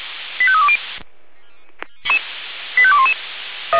Five-tone selective call (ZVEI), heard as quick runs of five tones, tones of about 70 ms. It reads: 85320 85211